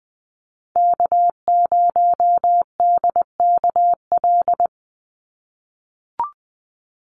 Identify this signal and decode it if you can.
Morse code, heard as a keyed tone, about 20 wpm, 700 Hz. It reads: K0DKL
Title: USA Call Signs 20wpm